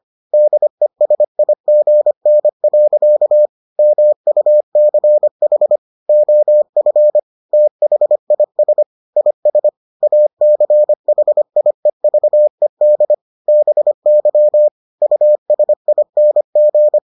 Morse code, heard as a keyed tone, about 25 wpm, 600 Hz. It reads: DESIGN. MUCH OF THIS IS ACHIEVED BY USING